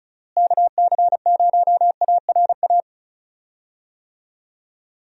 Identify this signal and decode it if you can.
Morse code, heard as a keyed tone, about 34 wpm, 700 Hz. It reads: KC0ARA
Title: USA Call Signs - 6 Characters - Repeated 2x - 35wpm